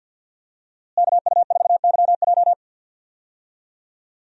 Morse code, heard as a keyed tone, about 52 wpm, 700 Hz. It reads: KU4YJ
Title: USA Call Signs - 5 Characters - Repeated 2x - 50wpm